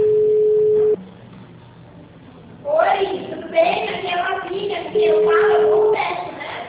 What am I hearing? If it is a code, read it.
Ringback tone (Europe, 425 Hz), with speech.